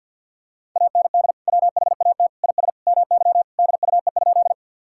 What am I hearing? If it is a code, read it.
Morse code, heard as a keyed tone, about 50 wpm, 700 Hz. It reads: AND WHAT IS MY DUE?
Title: Sentences from Top 1000 Words - Review Part 6 - 50wpm